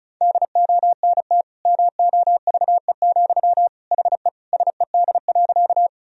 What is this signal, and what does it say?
Morse code, heard as a keyed tone, about 36 wpm, 700 Hz. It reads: DONT MOVE, HE SED.
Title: Sentences from Top 300 Words and Abbreviations 35wpm